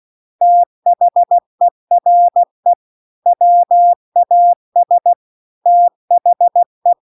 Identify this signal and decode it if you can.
Morse code, heard as a keyed tone, about 16 wpm, 700 Hz. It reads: THERE WAS THE
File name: 16WPM